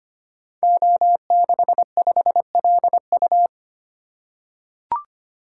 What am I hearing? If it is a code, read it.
Morse code, heard as a keyed tone, about 25 wpm, 700 Hz. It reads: O65LU
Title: Code Groups - 5 Letter:Numbers - 25wpm